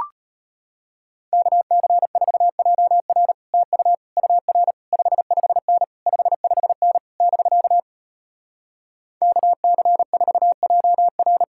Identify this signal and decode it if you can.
Morse code, heard as a keyed tone, about 33 wpm, 700 Hz. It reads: KC4JR TU UR 55N 55N <BK> KC4JR